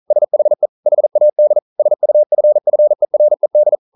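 Morse code, heard as a keyed tone, about 40 wpm, 600 Hz. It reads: SHE HAD SUFFERED
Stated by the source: Morse code practice at 41 wpm, from CW med Gnister.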